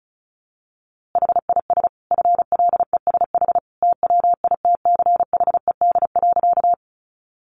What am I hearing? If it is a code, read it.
Morse code, heard as a keyed tone, about 34 wpm, 700 Hz. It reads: HIS FLESH TWITCHED.
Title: A Sound Of Thunder 35wpm